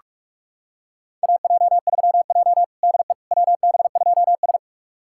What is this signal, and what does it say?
Morse code, heard as a keyed tone, about 42 wpm, 700 Hz. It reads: AJ3J DE WB2S
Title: USA Callsign DE Callsign - 4 Characters - 45wpm